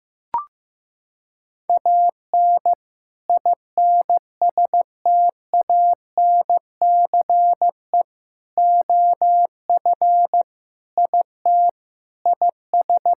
Morse code, heard as a keyed tone, about 15 wpm, 700 Hz. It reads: AN INSTANCE OF IT IS
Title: Sentences from Top 1000 Words - Review Part 5 - 15wpm